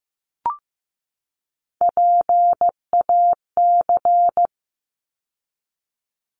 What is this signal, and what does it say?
Morse code, heard as a keyed tone, about 15 wpm, 700 Hz. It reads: PAC
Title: ARRL Sections 15wpm